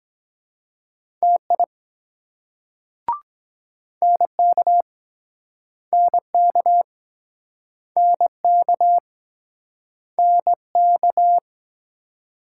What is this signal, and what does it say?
Morse code, heard as a keyed tone, about 24 wpm, 700 Hz. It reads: TI NK NK NK NK